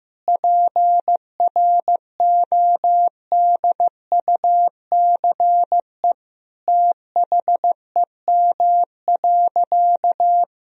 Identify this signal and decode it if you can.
Morse code, heard as a keyed tone, about 15 wpm, 700 Hz. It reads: PRODUCE THEM.
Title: Sentences from Top 1000 Words - Review Part 3 - 15wpm